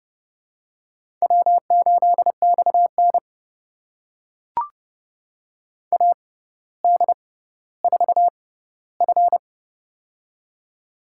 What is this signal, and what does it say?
Morse code, heard as a keyed tone, about 30 wpm, spaced as for 15 wpm, 700 Hz. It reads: W8XN AD4F
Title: USA Call Signs - 4 Characters - 6x Character Spacing - 30wpm